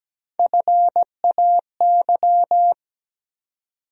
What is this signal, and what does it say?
Morse code, heard as a keyed tone, about 17 wpm, 700 Hz. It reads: FAY
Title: USA Call Sign Postfixes - 3 Letters 17wpm